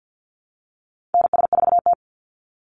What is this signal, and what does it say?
Morse code, heard as a keyed tone, about 50 wpm, 700 Hz. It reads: NS4T
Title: Call Signs Repeated 3x - 50wpm